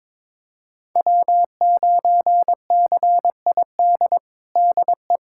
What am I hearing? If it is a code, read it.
Morse code, heard as a keyed tone, about 21 wpm, 700 Hz. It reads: W9CID DE
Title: USA Callsign DE Callsign - 5 Characters - 22wpm